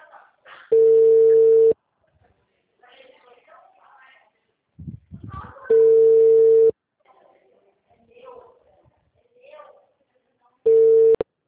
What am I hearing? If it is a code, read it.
Ringback tone (China, 450 Hz), with speech.